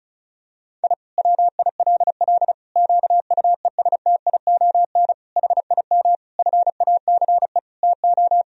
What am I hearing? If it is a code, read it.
Morse code, heard as a keyed tone, about 34 wpm, 700 Hz. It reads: I WILL QUESTION HIM FACE TO
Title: Sentences from Top 1000 Words - Review Part 2 - 35wpm